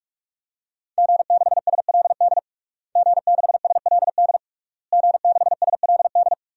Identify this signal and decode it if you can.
Morse code, heard as a keyed tone, about 46 wpm, 700 Hz. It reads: G6SLD G6SLD G6SLD
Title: DXCC-100 Call Signs - 5 Characters - Repeated 3x - 45wpm